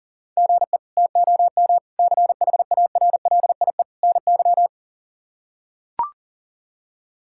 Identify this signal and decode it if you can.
Morse code, heard as a keyed tone, about 37 wpm, 700 Hz. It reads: GE TOM CHARLIE NY
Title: K1USN SST QSO Practice - 40wpm